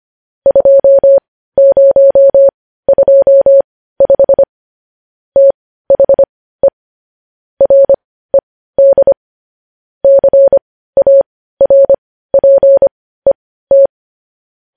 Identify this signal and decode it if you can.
Morse code, heard as a keyed tone, about 25 wpm, spaced as for 15 wpm, 550 Hz. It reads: 2025 THE RED CARPET
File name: podcast / News headlines-1x-25x15